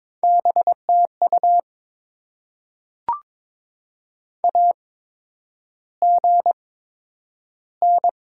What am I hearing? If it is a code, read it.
Morse code, heard as a keyed tone, about 22 wpm, spaced as for 9 wpm, 700 Hz. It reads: BTU AGN
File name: Top 20 QSO Elements - 8x Character Spacing - 22wpm